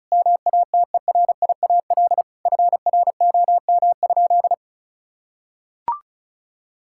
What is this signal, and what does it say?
Morse code, heard as a keyed tone, about 36 wpm, 700 Hz. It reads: MATERIAL FROM?
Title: Sentences from Top 1000 Words - Review Part 5 - 35wpm